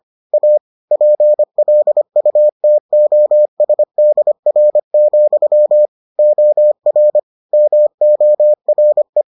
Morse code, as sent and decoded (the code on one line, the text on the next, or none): .- / .--. .-.. ..- - --- ... -.. .-. --..-- / --- .-. / -- --- .-. .
A PLUTOSDR, OR MORE